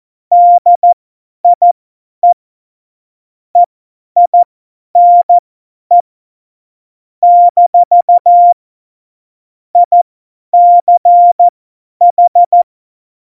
Morse code, heard as a keyed tone, about 14 wpm, spaced as for 10 wpm, 700 Hz. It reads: DIE EINE - ICH
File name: Morsekurs / Witze / Witz 10 April 2018, 50 F70